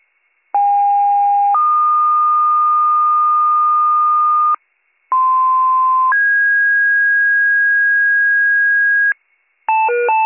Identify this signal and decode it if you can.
Two-tone page, heard as a one-second tone, then a three-second tone, about 810 then 1220 Hz, about 1040 then 1650 Hz, with noise. It is heard from 0.5 to 9.1 s.